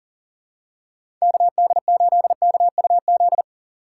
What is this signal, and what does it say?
Morse code, heard as a keyed tone, about 40 wpm, 700 Hz. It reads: KD8KUZ
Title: USA Call Signs - 6 Characters - 6x Character Spacing - 40wpm